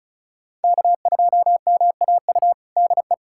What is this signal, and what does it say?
Morse code, heard as a keyed tone, about 36 wpm, 700 Hz. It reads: K2MAU DE
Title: USA Callsign DE Callsign 35wpm